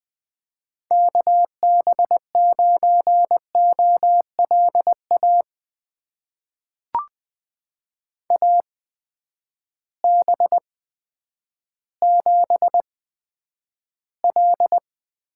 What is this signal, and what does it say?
Morse code, heard as a keyed tone, about 20 wpm, spaced as for 10 wpm, 700 Hz. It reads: KB9OLA AB7L
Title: USA Call Signs - 8x Character Spacing - 20wpm